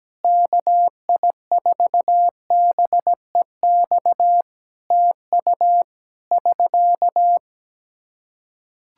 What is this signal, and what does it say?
Morse code, heard as a keyed tone, about 17 wpm, 700 Hz. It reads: KI4BEX TU <SK>
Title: K1USN SST QSO Practice 17wpm